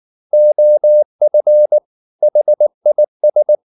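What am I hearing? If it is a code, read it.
Morse code, heard as a keyed tone, about 19 wpm, 600 Hz. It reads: OF HIS
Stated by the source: Morse code at 19 wpm from CW med Gnister, reading Ken Follett's novel Never in English.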